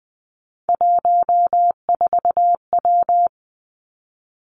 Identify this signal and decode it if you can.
Morse code, heard as a keyed tone, about 20 wpm, 700 Hz. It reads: 14W
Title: Three Challenging Letter-Number Combinations 20wpm